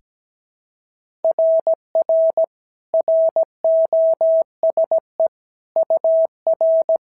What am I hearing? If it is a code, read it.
Morse code, heard as a keyed tone, about 16 wpm, 650 Hz. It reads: RR ROSE UR
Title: SOTA QSO Practice 17wpm